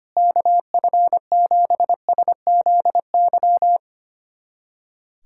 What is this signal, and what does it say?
Morse code, heard as a keyed tone, about 25 wpm, 700 Hz. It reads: KF7SZY
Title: USA Callsign DE Callsign - 6 Characters - 25wpm